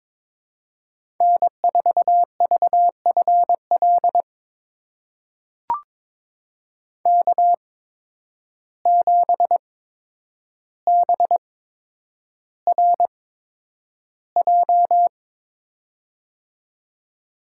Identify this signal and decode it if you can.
Morse code, heard as a keyed tone, about 21 wpm, spaced as for 9 wpm, 700 Hz. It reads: N4VFL K7BRJ
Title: USA Call Signs - 5 Characters - 8x Character Spacing - 22wpm